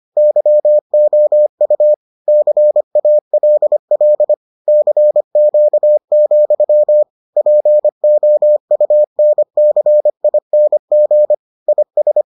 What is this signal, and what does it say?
Morse code, heard as a keyed tone, about 25 wpm, 600 Hz. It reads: YOU CALL CQ, POUNCING IS